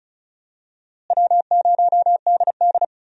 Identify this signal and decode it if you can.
Morse code, heard as a keyed tone, about 36 wpm, 700 Hz. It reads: W0DD
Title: USA Call Signs - 4 Characters - Repeated 2x - 35wpm